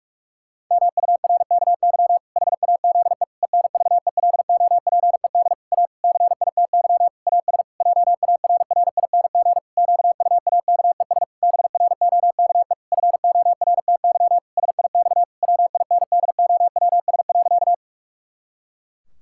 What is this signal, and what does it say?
Morse code, heard as a keyed tone, about 46 wpm, 700 Hz. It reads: MURKY HAZE ENVELOPED A CITY AS JARRING QUAKES BROKE FORTY SIX WINDOWS.
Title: Pangram Challenge v2 - 45wpm